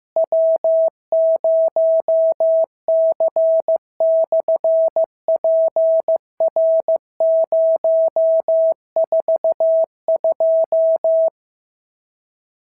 Morse code, heard as a keyed tone, about 15 wpm, 650 Hz. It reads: W0C/PR042
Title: SOTA QSO Practice 15wpm